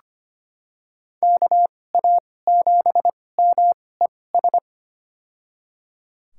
Morse code, heard as a keyed tone, about 25 wpm, spaced as for 18 wpm, 700 Hz. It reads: KA7MES
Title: USA Call Signs - 6 Characters - 2x Character Spacing - 25wpm